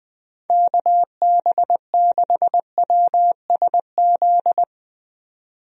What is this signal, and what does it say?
Morse code, heard as a keyed tone, about 20 wpm, 700 Hz. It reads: KB6WSZ
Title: USA Call Signs - 6 Characters - 20wpm